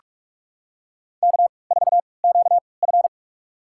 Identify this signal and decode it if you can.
Morse code, heard as a keyed tone, about 42 wpm, spaced as for 25 wpm, 700 Hz. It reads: K4QF
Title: USA Call Signs - 3x Character Spacing - 45wpm